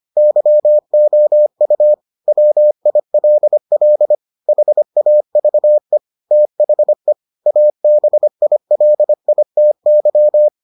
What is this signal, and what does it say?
Morse code, heard as a keyed tone, about 25 wpm, 600 Hz. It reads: YOU WILL HAVE THE ABILITY